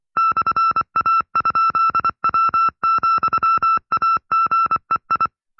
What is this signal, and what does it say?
Morse code, heard as a keyed tone, about 24 wpm, 1330 Hz. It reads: /A?W,AGEI
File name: FreeFiles / CW-practice / 20WPM-side2